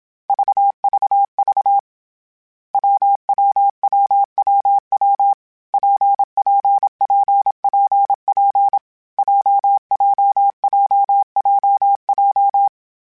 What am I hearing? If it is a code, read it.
Morse code, heard as a keyed tone, about 27 wpm, 800 Hz. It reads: VVV WWWWW PPPPP JJJJJ
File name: Variable Speed / Variable Speed-mp3s / WPJ WPJ1 sp @25